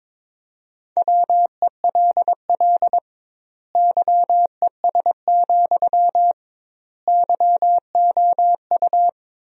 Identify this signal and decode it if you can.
Morse code, heard as a keyed tone, about 21 wpm, 700 Hz. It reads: WELL YES, YOU
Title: 2x Word-Spacing - Sentences from Top 500 Words - 22wpm